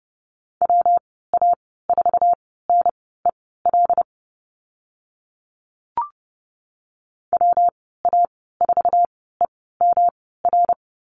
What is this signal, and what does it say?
Morse code, heard as a keyed tone, about 30 wpm, spaced as for 17 wpm, 700 Hz. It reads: WA4NEL WA4EMR
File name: Lesson 19g - Letter w - Call Signs 30wpm